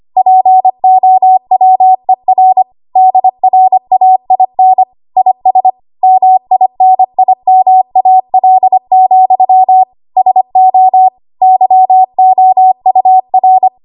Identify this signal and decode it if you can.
Morse code, heard as a keyed tone, about 25 wpm, 750 Hz. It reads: POWER DRAIN IS MINIMAL, SO YOUR
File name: Archive / 25 WPM / 25WPM